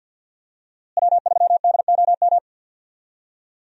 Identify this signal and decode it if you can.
Morse code, heard as a keyed tone, about 52 wpm, 700 Hz. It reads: W3DOM
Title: USA Call Signs - 5 Characters - Repeated 2x - 50wpm